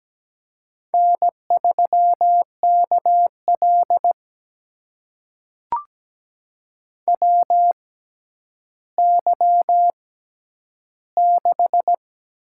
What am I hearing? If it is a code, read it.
Morse code, heard as a keyed tone, about 17 wpm, spaced as for 10 wpm, 700 Hz. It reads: N3KL WY6
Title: USA Call Signs - 4 Characters - 6x Character Spacing - 17wpm